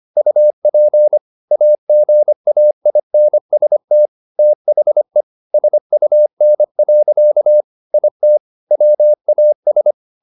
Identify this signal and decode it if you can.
Morse code, heard as a keyed tone, about 25 wpm, 600 Hz. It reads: UP AGAINST THE SUN. IT WAS
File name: the old man and the sea part 1 25wpm